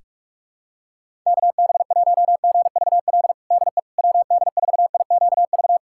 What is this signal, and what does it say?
Morse code, heard as a keyed tone, about 46 wpm, 700 Hz. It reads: KB1GVL DE WD4IQV
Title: USA Callsign DE Callsign - 6 Characters - 45wpm